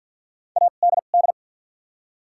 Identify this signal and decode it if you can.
Morse code, heard as a keyed tone, about 53 wpm, spaced as for 37 wpm, 700 Hz. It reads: ADD
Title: 2x Character Spacing - Top 200 Words - 50wpm